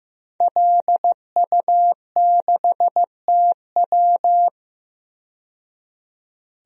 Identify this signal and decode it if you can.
Morse code, heard as a keyed tone, about 15 wpm, 700 Hz. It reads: LU6TW
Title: DXCC-100 Call Signs - 5 Characters - 15wpm